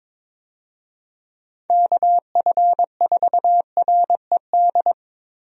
Morse code, heard as a keyed tone, about 22 wpm, 700 Hz. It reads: KF4RED
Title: USA Call Signs - 6 Characters - Repeated 3x - 22wpm